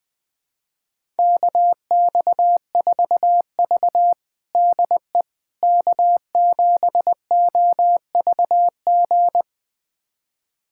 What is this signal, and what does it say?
Morse code, heard as a keyed tone, about 20 wpm, 700 Hz. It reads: KX4V DE K7OVG
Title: USA Callsign DE Callsign 20wpm